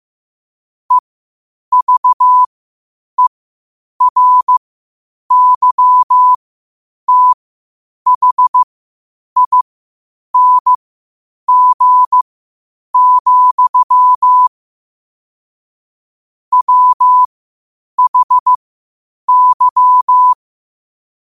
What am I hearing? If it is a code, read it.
Morse code, heard as a keyed tone, about 15 wpm, spaced as for 8 wpm, 1000 Hz. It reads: EVERYTHING, WHY
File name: qotdfiles / 8wpm